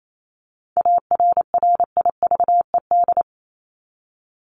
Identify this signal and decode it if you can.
Morse code, heard as a keyed tone, about 28 wpm, 700 Hz. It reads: ARRIVED
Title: ED Suffix - Top 100 Words with Base Words 28wpm